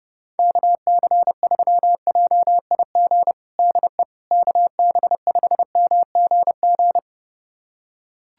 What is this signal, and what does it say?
Morse code, heard as a keyed tone, about 30 wpm, 700 Hz. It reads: KC3JIG DE KB5MGG
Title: USA Callsign DE Callsign 30wpm